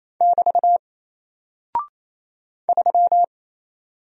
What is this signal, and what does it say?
Morse code, heard as a keyed tone, about 28 wpm, 700 Hz. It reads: <BT> 3
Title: Single Character (Letter:Number:Punctuation:Prosign) Alt 28wpm